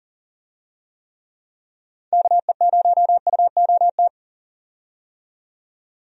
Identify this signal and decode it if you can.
Morse code, heard as a keyed tone, about 40 wpm, 700 Hz. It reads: KE0UOT
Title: USA Call Signs - 6 Characters - 40wpm